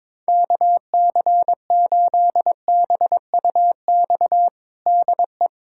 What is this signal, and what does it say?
Morse code, heard as a keyed tone, about 22 wpm, 700 Hz. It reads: KC8BUX DE
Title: USA Callsign DE Callsign - 6 Characters - 22wpm